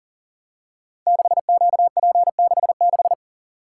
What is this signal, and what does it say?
Morse code, heard as a keyed tone, about 42 wpm, 690 Hz. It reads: 6QP66
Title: Code Groups - 5 Letter:Numbers - 40wpm